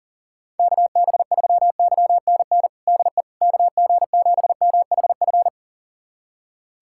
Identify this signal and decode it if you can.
Morse code, heard as a keyed tone, about 40 wpm, 700 Hz. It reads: KB3YNN DE KG7MHF
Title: USA Callsign DE Callsign - 6 Characters - 40wpm